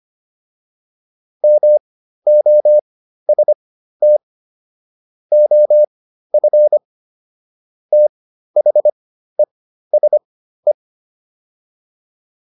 Morse code, heard as a keyed tone, about 25 wpm, spaced as for 13 wpm, 600 Hz. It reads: MOST OF THESE
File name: PR405 13